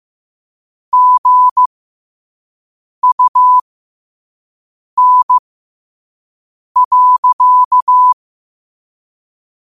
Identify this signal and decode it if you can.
Morse code, heard as a keyed tone, about 15 wpm, spaced as for 5 wpm, 1000 Hz. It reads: GUN.
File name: qotdfiles / 5wpm